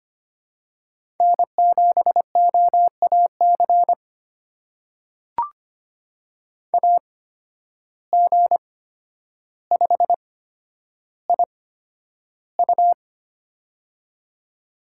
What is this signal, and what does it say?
Morse code, heard as a keyed tone, about 25 wpm, spaced as for 11 wpm, 700 Hz. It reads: N7OAC AG5IU
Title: USA Call Signs - 8x Character Spacing - 25wpm